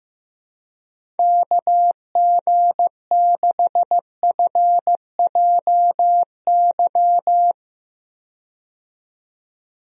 Morse code, heard as a keyed tone, about 15 wpm, 700 Hz. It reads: KG6FJY
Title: USA Call Signs - 6 Characters - 15wpm